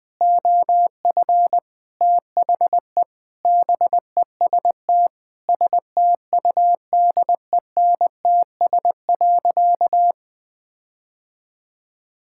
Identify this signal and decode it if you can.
Morse code, heard as a keyed tone, about 20 wpm, 700 Hz. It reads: OF THE BEST STUDENTS.
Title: Sentences with 4-Word Phrases - 20wpm